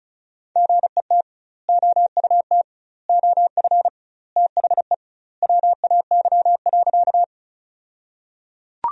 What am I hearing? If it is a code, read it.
Morse code, heard as a keyed tone, about 34 wpm, 700 Hz. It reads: GET OUT OF THE WAY.
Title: 2x Word-Spacing - Sentences from Top 200 Words - 35wpm